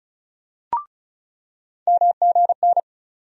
Morse code, heard as a keyed tone, about 36 wpm, 700 Hz. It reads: MGN
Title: USA Call Sign Postfixes - 3 Letters 35wpm